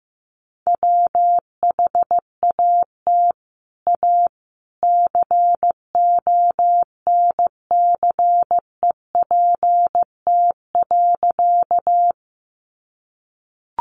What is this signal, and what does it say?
Morse code, heard as a keyed tone, about 15 wpm, 700 Hz. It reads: WHAT A CONCEPT.
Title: Sentences from Top 1100 Words 15wpm